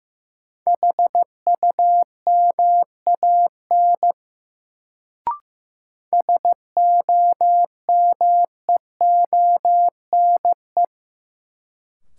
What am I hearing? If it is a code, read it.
Morse code, heard as a keyed tone, about 15 wpm, 700 Hz. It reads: HUMAN SOMEONE
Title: Sentences from Top 600 Words - New Word Drill - 15wpm